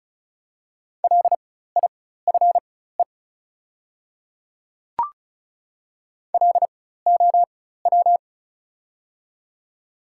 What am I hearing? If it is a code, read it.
Morse code, heard as a keyed tone, about 36 wpm, spaced as for 16 wpm, 690 Hz. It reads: LIFE LOW
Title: 4x Character Spacing - Top 300 Words - 35wpm